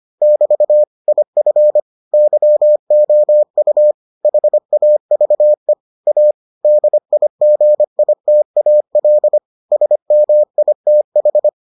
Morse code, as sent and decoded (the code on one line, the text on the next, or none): -...- / .. ..-. / -.-- --- ..- / .... .- ...- . / .- / -.. .. --. .. - .- .-.. / ... -- .. - ....
<BT> IF YOU HAVE A DIGITAL SMITH